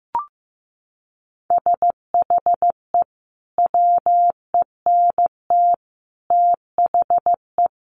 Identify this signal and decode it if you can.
Morse code, heard as a keyed tone, about 14 wpm, 700 Hz. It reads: SHE WENT THE
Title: Sentences from Top 1500 Words 15wpm